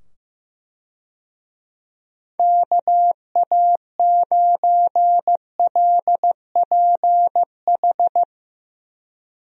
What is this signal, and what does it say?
Morse code, heard as a keyed tone, about 15 wpm, 700 Hz. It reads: KA9LPH
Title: USA Call Signs - 6 Characters - 15wpm